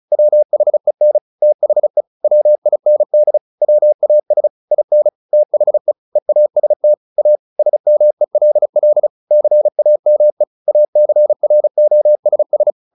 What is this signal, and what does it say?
Morse code, heard as a keyed tone, about 34 wpm, 600 Hz. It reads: WHEN THE WIND WAS IN THE EAST A SMELL CAME ACROSS
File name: the old man and the sea / the old man and the sea 1 35wpm